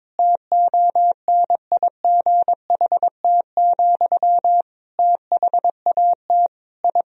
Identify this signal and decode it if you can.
Morse code, heard as a keyed tone, about 22 wpm, 700 Hz. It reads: TONIGHT, THAT I
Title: Sentences from Top 1000 Words - Review Part 3 - 22wpm